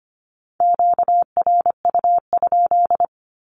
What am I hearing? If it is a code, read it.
Morse code, heard as a keyed tone, about 25 wpm, 700 Hz. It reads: QRU?
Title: Q-Codes with Spoken Meaning - 25wpm